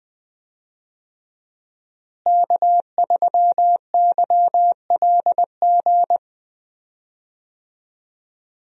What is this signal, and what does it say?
Morse code, heard as a keyed tone, about 20 wpm, 700 Hz. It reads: K3YLG
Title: USA Call Signs - 5 Characters - Repeated 2x - 20wpm